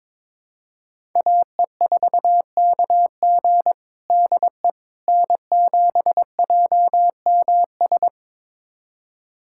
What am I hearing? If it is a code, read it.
Morse code, heard as a keyed tone, about 22 wpm, 700 Hz. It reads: AE4KG DE N7JMS
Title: USA Callsign DE Callsign - 5 Characters - 22wpm